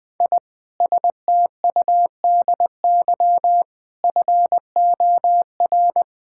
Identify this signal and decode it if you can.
Morse code, heard as a keyed tone, about 20 wpm, 700 Hz. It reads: I STUDY FOR
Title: Sentences with 7-Word Phrases - Spoken Beginning - 20wpm